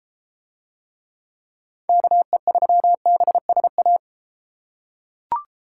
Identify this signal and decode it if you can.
Morse code, heard as a keyed tone, about 32 wpm, 700 Hz. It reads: KE3BSA